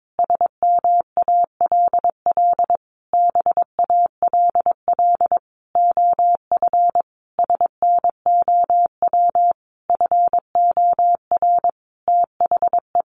Morse code, heard as a keyed tone, about 22 wpm, 700 Hz. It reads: SMALL BALL OF SNOW FOR THE
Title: Easy Phrases 22wpm